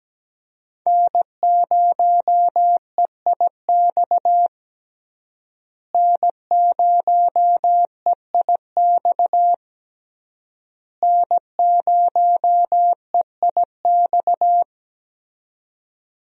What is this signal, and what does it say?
Morse code, heard as a keyed tone, about 17 wpm, 700 Hz. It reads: N0EIX N0EIX N0EIX
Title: USA Call Signs - 5 Characters - Repeated 3x - 17wpm